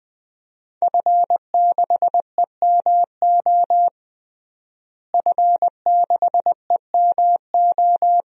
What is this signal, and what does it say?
Morse code, heard as a keyed tone, about 20 wpm, 700 Hz. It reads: F6EMO F6EMO
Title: DXCC-100 Call Signs - 5 Characters - Repeated 3x - 20wpm